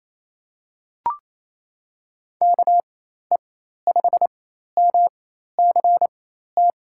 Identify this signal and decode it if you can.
Morse code, heard as a keyed tone, about 28 wpm, spaced as for 13 wpm, 700 Hz. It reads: KE5MCT
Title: USA Call Signs - 6 Characters - 4x Character Spacing - 28wpm